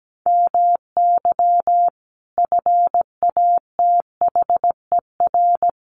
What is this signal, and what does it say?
Morse code, heard as a keyed tone, about 17 wpm, 700 Hz. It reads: MY FATHER
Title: Sentences from Top 1500 Words 17wpm